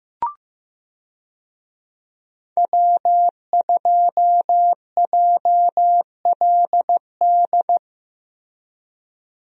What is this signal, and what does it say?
Morse code, heard as a keyed tone, about 15 wpm, 700 Hz. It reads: W2JLD
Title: USA Call Signs - 5 Characters - Repeated 2x - 15wpm